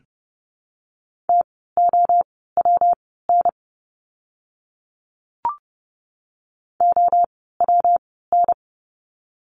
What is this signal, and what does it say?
Morse code, heard as a keyed tone, about 30 wpm, spaced as for 17 wpm, 700 Hz. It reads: TOWN OWN
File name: course / 30:10wpm / 115 Lesson 19f - Letter w - Words 30wpm